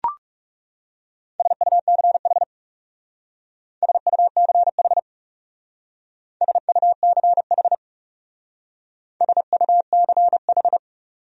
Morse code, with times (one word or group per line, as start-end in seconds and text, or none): 1.4-2.4 SUCH
3.8-5.0 SUCH
6.4-7.7 SUCH
9.2-10.8 SUCH